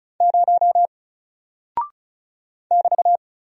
Morse code, heard as a keyed tone, about 36 wpm, 700 Hz. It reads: 0 <BT>
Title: Single Character (Letter:Number:Punctuation:Prosign) Alt 35wpm